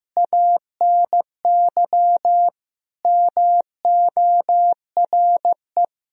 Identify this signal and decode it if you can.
Morse code, heard as a keyed tone, about 16 wpm, 700 Hz. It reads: ANY MORE
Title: Sentences from Top 1000 Words - Review Part 4 - 15wpm